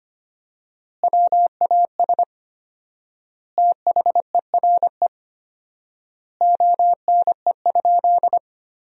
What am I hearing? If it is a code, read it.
Morse code, heard as a keyed tone, about 25 wpm, 700 Hz. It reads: WAS THERE ONE?
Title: 4x Word-Spacing - Sentences from Top 100 Words - 25wpm